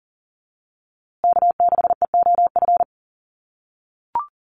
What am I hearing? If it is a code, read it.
Morse code, heard as a keyed tone, about 40 wpm, 700 Hz. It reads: K6EOF
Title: USA Call Signs Repeated 3x 40wpm